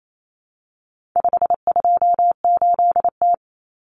Morse code, heard as a keyed tone, about 28 wpm, 700 Hz. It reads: 528T
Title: Common Cut Numbers - 4-Digit 28wpm